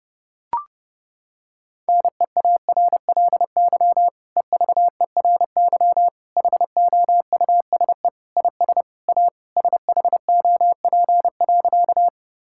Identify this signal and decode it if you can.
Morse code, heard as a keyed tone, about 30 wpm, 700 Hz. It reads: NEARLY EVERY HOUSE IS A SHOP.
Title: Sentences from Top 1600 Words 30wpm